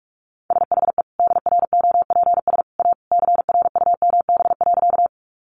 Morse code, heard as a keyed tone, about 44 wpm, 700 Hz. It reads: SHE DROPS A CRUMB.
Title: Easy Phrases 45wpm